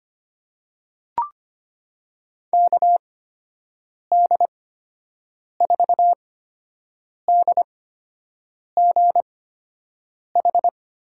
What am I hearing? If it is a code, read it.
Morse code, heard as a keyed tone, about 25 wpm, spaced as for 7 wpm, 700 Hz. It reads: KD4DGH
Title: USA Call Signs - 6 Characters - 8x Character Spacing - 25wpm